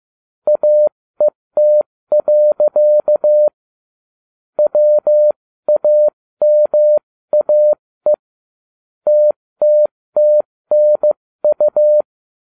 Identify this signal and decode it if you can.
Morse code, heard as a keyed tone, about 15 wpm, 600 Hz. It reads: AET. WAMAE TTTNU